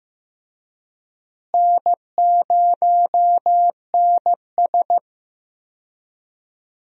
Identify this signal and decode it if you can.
Morse code, heard as a keyed tone, about 16 wpm, 700 Hz. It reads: N0NS
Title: USA Call Signs - 4 Characters - Repeated 2x - 15wpm